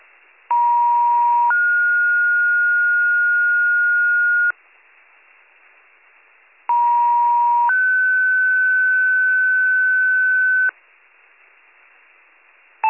Two-tone page, heard as a one-second tone, then a three-second tone, about 950 then 1430 Hz, about 950 then 1500 Hz. It is heard from 0.5 to 10.7 s.